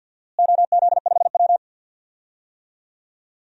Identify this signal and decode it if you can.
Morse code, heard as a keyed tone, about 50 wpm, 700 Hz. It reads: OZ5W
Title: DXCC-100 Call Signs - 4 Characters - Repeated 2x - 50wpm